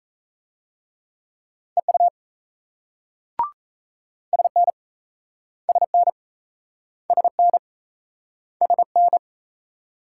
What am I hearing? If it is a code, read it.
Morse code, heard as a keyed tone, about 38 wpm, 700 Hz. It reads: EU SN SN SN SN